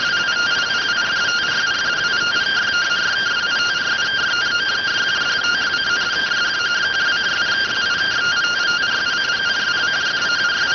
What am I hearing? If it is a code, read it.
Two-tone teleprinter signal, about 170 Hz shift.